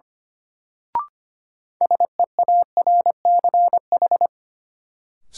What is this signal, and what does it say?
Morse code, heard as a keyed tone, about 25 wpm, 700 Hz. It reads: SEARCH
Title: Sentences from Top 800 Words - New Word Drill - 25wpm